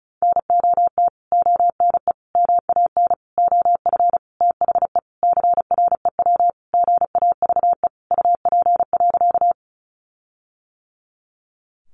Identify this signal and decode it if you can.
Morse code, heard as a keyed tone, about 34 wpm, 700 Hz. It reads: NOT ONE MAN OF THE CREW GAVE UP.
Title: Sentences from Top 1500 Words 35wpm